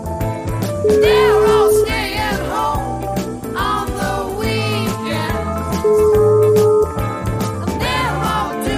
Ringback tone (Europe, 425 Hz), with noise.